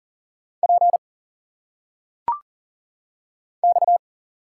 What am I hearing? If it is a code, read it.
Morse code, heard as a keyed tone, about 40 wpm, 700 Hz. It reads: P X
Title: Single Letters - Phonetic Pronunciation 40wpm